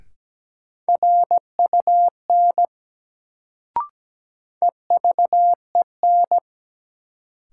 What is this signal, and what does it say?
Morse code, heard as a keyed tone, about 17 wpm, 700 Hz. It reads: RUN EVEN
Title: Sets of 4 Words - New Word Drill - 17wpm